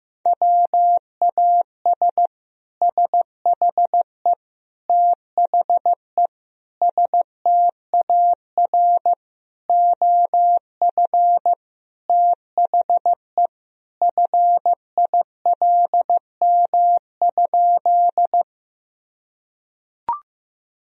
Morse code, heard as a keyed tone, about 15 wpm, 700 Hz. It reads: WAS SHE THE STAR OF THE FILM?
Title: Sentences from Top 1000 Words - Review Part 3 - 15wpm